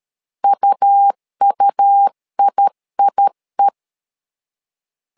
Morse code, heard as a keyed tone, about 13 wpm, 800 Hz. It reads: UUIIE
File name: LEZIONI / 2 UHDN / 2 UH 12wpm 1439